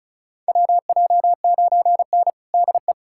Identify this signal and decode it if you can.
Morse code, heard as a keyed tone, about 36 wpm, 700 Hz. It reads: WJ9N DE
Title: USA Callsign DE Callsign - 4 Characters - 35wpm